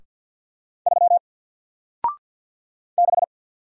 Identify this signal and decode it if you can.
Morse code, heard as a keyed tone, about 48 wpm, 700 Hz. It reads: 3 6
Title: Single Letter-Number Alt 50wpm